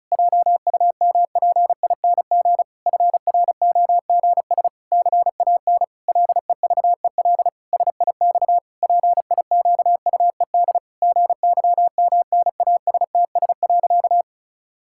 Morse code, heard as a keyed tone, about 36 wpm, 700 Hz. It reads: JUMPING FROGS CAN LEVEL SIX PIQUED GYMNASTS.
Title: Pangram Challenge v2 - 35wpm